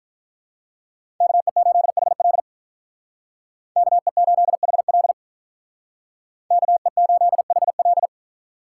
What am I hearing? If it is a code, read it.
Morse code, heard as a keyed tone, about 46 wpm, 700 Hz. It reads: KE8HL KE8HL KE8HL